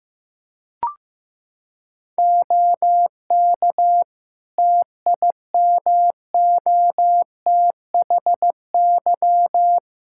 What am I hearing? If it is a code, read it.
Morse code, heard as a keyed tone, about 16 wpm, 700 Hz. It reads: OK TIMOTHY
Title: Practice QSO - Exchange 2 - Info - 15wpm